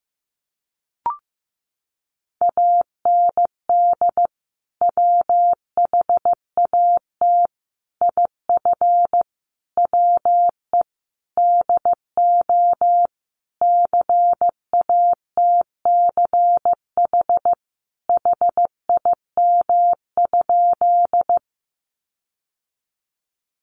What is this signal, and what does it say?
Morse code, heard as a keyed tone, about 15 wpm, 700 Hz. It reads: AND WHAT IF WE DO CATCH HIM?
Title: Sentences from Top 1500 Words 15wpm